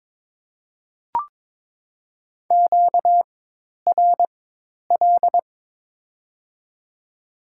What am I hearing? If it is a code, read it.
Morse code, heard as a keyed tone, about 22 wpm, spaced as for 10 wpm, 700 Hz. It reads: QRL
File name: Top 20 QSO Elements - 4x Character Spacing - 22wpm